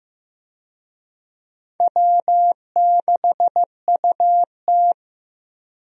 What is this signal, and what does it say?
Morse code, heard as a keyed tone, about 15 wpm, 700 Hz. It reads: W6UT
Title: USA Call Signs - 4 Characters - Repeated 2x - 15wpm